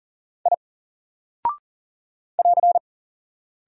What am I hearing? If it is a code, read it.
Morse code, heard as a keyed tone, about 39 wpm, 700 Hz. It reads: I <AR>
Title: Single Character (Letter:Number:Punctuation:Prosign) 40wpm